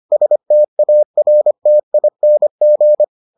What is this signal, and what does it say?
Morse code, heard as a keyed tone, about 25 wpm, 600 Hz. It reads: STARTING